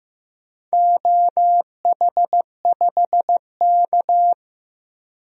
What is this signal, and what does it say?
Morse code, heard as a keyed tone, about 15 wpm, 700 Hz. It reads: OH5K
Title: DXCC-100 Call Signs - 4 Characters - 15wpm